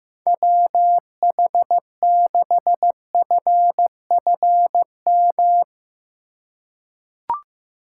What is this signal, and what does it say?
Morse code, heard as a keyed tone, about 15 wpm, 700 Hz. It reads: WH6FFM
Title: USA Call Signs - 6 Characters - 4x Character Spacing - 15wpm